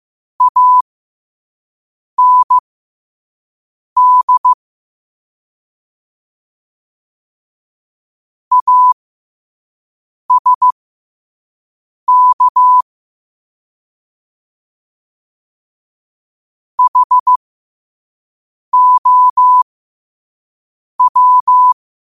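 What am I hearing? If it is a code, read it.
Morse code, heard as a keyed tone, about 15 wpm, spaced as for 5 wpm, 1000 Hz. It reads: AND ASK HOW
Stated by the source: Morse code practice at 5 wpm.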